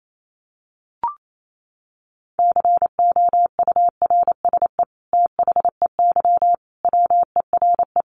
Morse code, heard as a keyed tone, about 28 wpm, 700 Hz. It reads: COURSE THEY WERE
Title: Sentences from Top 900 Words 28wpm